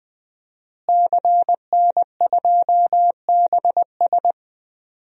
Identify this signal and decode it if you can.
Morse code, heard as a keyed tone, about 20 wpm, 700 Hz. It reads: CN2BS
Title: DXCC-100 Call Signs - 5 Characters - Repeated 2x - 20wpm